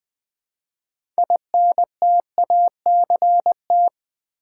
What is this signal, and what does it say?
Morse code, heard as a keyed tone, about 20 wpm, 700 Hz. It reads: INTACT
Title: 6 Letter Words - First 3 Letters Given - 20wpm